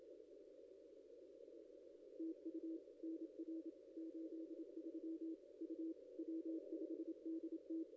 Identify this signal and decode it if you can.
Morse code, heard as a keyed tone, about 25 wpm, 320 Hz. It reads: TU NR 83 U W5DT
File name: contests / ss-cw-k5tr-op-n5rz -0900